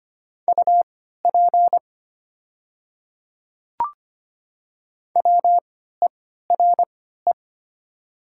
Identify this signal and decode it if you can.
Morse code, heard as a keyed tone, about 25 wpm, spaced as for 14 wpm, 700 Hz. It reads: UP WERE
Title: 3x Character Spacing - Top 100 Words - 25wpm